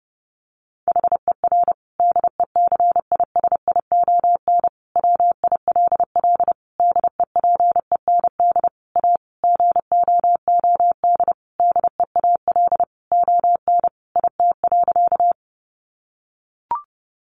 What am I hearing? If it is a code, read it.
Morse code, heard as a keyed tone, about 30 wpm, 700 Hz. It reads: HER DECISION WILL DEPEND A GOOD DEAL ON IT.
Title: Sentences from Top 1300 Words 30wpm